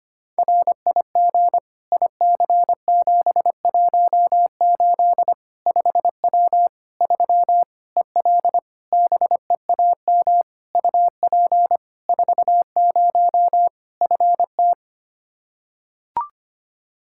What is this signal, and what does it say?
Morse code, heard as a keyed tone, about 25 wpm, 700 Hz. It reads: RIG IC718 5W 3 EL BEAM UP 40 FT
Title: K1USN SST QSO Practice 25wpm